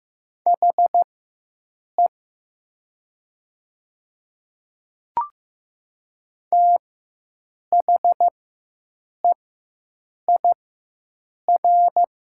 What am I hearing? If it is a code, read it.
Morse code, heard as a keyed tone, about 15 wpm, spaced as for 7 wpm, 700 Hz. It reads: HE THEIR
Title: 4x Character Spacing - Top 100 Words - 15wpm